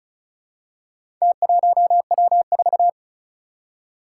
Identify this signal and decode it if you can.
Morse code, heard as a keyed tone, about 36 wpm, 700 Hz. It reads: T1W4
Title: Four Challenging Letter-Number Combinations 35wpm